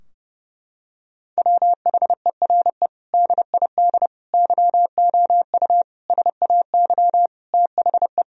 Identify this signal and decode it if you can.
Morse code, heard as a keyed tone, about 32 wpm, 700 Hz. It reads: WHERE DID YOU SAY THE
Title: Sentences from Top 1000 Words - Review Part 3 - 30wpm